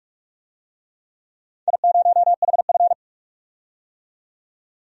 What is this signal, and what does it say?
Morse code, heard as a keyed tone, about 44 wpm, 700 Hz. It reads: I0HF
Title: DXCC-100 Call Signs - 4 Characters - 45wpm